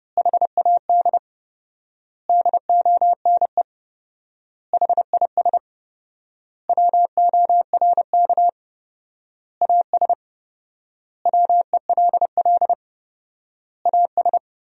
Morse code, heard as a keyed tone, about 30 wpm, 700 Hz. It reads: HAD DONE HIS WORK AS WELL AS
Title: 4x Word-Spacing - Sentences from Top 500 Words - 30wpm